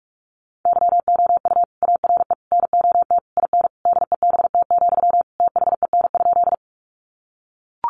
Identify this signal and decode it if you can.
Morse code, heard as a keyed tone, about 44 wpm, 700 Hz. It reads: YOU ARE NOT IN DEBT, THEN?
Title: Sentences from Top 1300 Words 45wpm